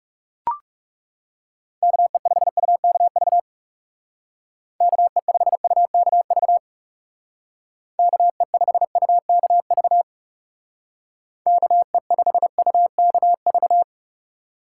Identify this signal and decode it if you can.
Morse code, heard as a keyed tone, about 39 wpm, 700 Hz. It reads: KE5UKV KE5UKV KE5UKV KE5UKV